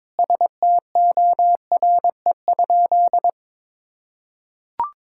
Morse code, heard as a keyed tone, about 22 wpm, 700 Hz. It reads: STORE?
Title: Sentences from Top 1000 Words - Review Part 5 - 22wpm